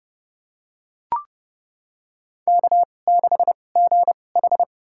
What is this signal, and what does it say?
Morse code, heard as a keyed tone, about 30 wpm, spaced as for 22 wpm, 700 Hz. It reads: K6GH
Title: USA Call Signs - 4 Characters - 2x Character Spacing - 30wpm